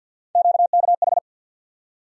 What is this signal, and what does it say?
Morse code, heard as a keyed tone, about 52 wpm, 700 Hz. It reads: QKH
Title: USA Call Sign Postfixes - 3 Letters 50wpm